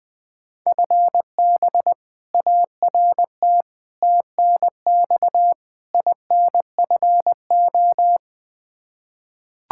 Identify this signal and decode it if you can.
Morse code, heard as a keyed tone, about 20 wpm, 700 Hz. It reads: FB ART TNX INFO
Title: K1USN SST QSO Practice 20wpm